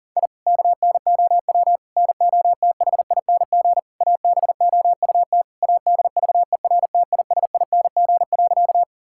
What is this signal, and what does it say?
Morse code, heard as a keyed tone, about 37 wpm, 700 Hz. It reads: I KNOW NOTHING ABOUT ADVERTISING.
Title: Sentences from Top 1000 Words - Review Part 6 - 40wpm